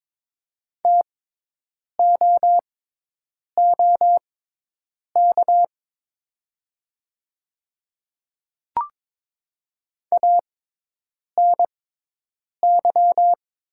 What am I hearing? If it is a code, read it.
Morse code, heard as a keyed tone, about 22 wpm, spaced as for 8 wpm, 700 Hz. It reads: TOOK ANY
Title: 6x Character Spacing - Top 300 Words - 22wpm